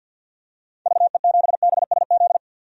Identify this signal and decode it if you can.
Morse code, heard as a keyed tone, about 52 wpm, 700 Hz. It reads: VE7BSZ
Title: DXCC-100 Call Signs - 6 Characters - 50wpm